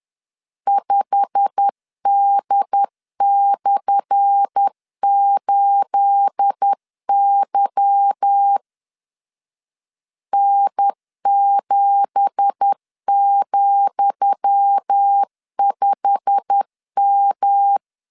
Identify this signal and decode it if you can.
Morse code, heard as a keyed tone, about 10 wpm, 800 Hz. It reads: 5D/8Y N7,5M
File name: LEZIONI / 10 SIGNS / 10D COMA 10wpm 3013